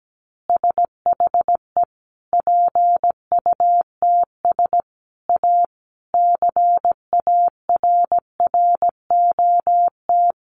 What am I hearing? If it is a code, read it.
Morse code, heard as a keyed tone, about 17 wpm, 700 Hz. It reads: SHE PUTS A CARROT
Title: Easy Phrases 17wpm